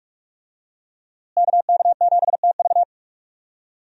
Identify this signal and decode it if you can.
Morse code, heard as a keyed tone, about 45 wpm, 700 Hz. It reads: KK7TV
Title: USA Call Signs - 5 Characters - Repeated 3x - 45wpm